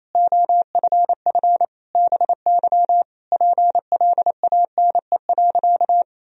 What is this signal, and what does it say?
Morse code, heard as a keyed tone, about 28 wpm, 700 Hz. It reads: OFF BY PLANE.
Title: Sentences from Top 1000 Words - Review Part 3 - 28wpm